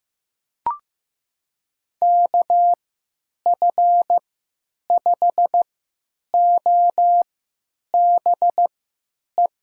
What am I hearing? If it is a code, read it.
Morse code, heard as a keyed tone, about 15 wpm, spaced as for 9 wpm, 700 Hz. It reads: KF5OBE
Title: USA Call Signs - 6 Characters - 3x Character Spacing - 15wpm